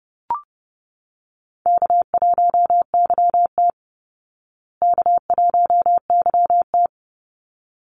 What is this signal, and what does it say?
Morse code, heard as a keyed tone, about 30 wpm, 700 Hz. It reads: K1YT K1YT
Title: Call Signs Repeated 3x - 30wpm